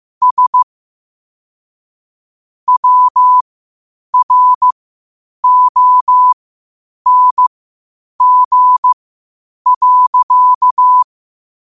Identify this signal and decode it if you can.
Morse code, heard as a keyed tone, about 15 wpm, spaced as for 8 wpm, 1000 Hz. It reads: S WRONG.
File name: qotdfiles / 8wpm